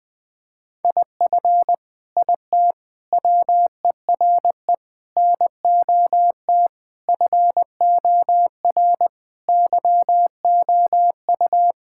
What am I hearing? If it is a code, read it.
Morse code, heard as a keyed tone, about 21 wpm, 700 Hz. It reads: IF IT WERE NOT FOR YOU
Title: Sentences with 5-Word Phrases - Spoken Beginning - 20wpm